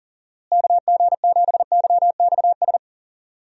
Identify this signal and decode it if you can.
Morse code, heard as a keyed tone, about 40 wpm, 700 Hz. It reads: KG7YXS
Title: USA Call Signs - 6 Characters - Repeated 3x - 40wpm